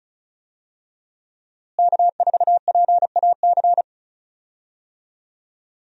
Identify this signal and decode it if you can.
Morse code, heard as a keyed tone, about 36 wpm, 700 Hz. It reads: K4PAC
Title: USA Call Signs - 5 Characters - Repeated 3x - 35wpm